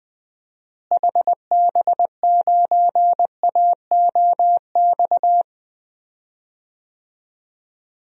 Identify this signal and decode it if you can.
Morse code, heard as a keyed tone, about 20 wpm, 690 Hz. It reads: HB9AOX
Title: DXCC-100 Call Signs - 6 Characters - Repeated 2x - 20wpm